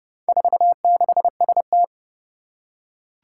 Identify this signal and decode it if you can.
Morse code, heard as a keyed tone, about 28 wpm, 700 Hz. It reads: 46ST
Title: Four Challenging Letter-Number Combinations 30wpm